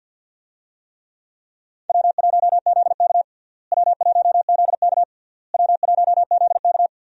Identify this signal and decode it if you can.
Morse code, heard as a keyed tone, about 52 wpm, 700 Hz. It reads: W1ZK W1ZK W1ZK
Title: USA Call Signs - 4 Characters - Repeated 3x - 50wpm